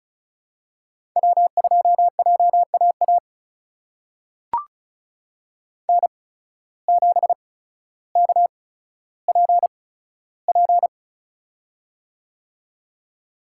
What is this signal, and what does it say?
Morse code, heard as a keyed tone, about 36 wpm, spaced as for 15 wpm, 700 Hz. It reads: W2JAA N7KPP
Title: USA Call Signs - 8x Character Spacing - 35wpm